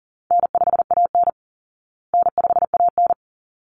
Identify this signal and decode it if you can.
Morse code, heard as a keyed tone, about 40 wpm, 700 Hz. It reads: N5AN N5AN
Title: Call Signs Repeated 3x - 40wpm